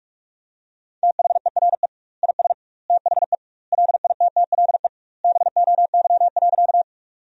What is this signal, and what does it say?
Morse code, heard as a keyed tone, about 45 wpm, 700 Hz. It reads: THERE IS THE LITTLE BOY.
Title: 2x Word-Spacing - Sentences from Top 200 Words - 45wpm